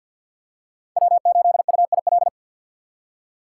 Morse code, heard as a keyed tone, about 51 wpm, 700 Hz. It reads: W8UIL
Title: K1USN SST QSO Practice - 50wpm